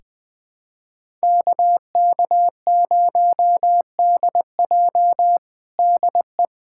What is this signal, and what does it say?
Morse code, heard as a keyed tone, about 20 wpm, 700 Hz. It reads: KK0DJ DE
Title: USA Callsign DE Callsign - 5 Characters - 20wpm